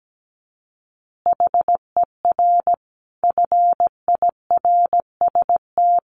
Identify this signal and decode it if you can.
Morse code, heard as a keyed tone, about 17 wpm, 700 Hz. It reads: HER FIRST
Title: Additional Easy Phrases 17wpm